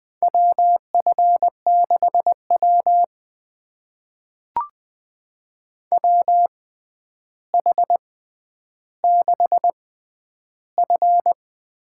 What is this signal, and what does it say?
Morse code, heard as a keyed tone, about 21 wpm, spaced as for 10 wpm, 700 Hz. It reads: WF6W WH6F
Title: USA Call Signs - 4 Characters - 6x Character Spacing - 20wpm